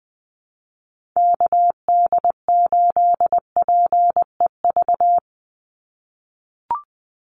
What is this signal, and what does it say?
Morse code, heard as a keyed tone, about 20 wpm, 700 Hz. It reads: KD8PEV
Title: USA Call Signs 20wpm